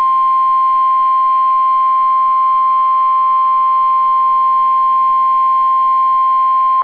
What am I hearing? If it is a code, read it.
Emergency Alert System (SAME) tones (the 1050 Hz attention tone). It is heard from 1.0 to 6.2 s.